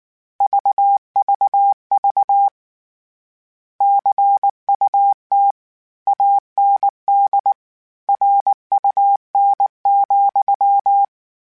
Morse code, heard as a keyed tone, about 19 wpm, 800 Hz. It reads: VVV CUT AND RUN,
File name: Word Pairs @18